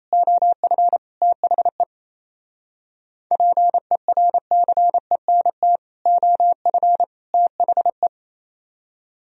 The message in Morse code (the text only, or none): OF THE PERCENT OF THE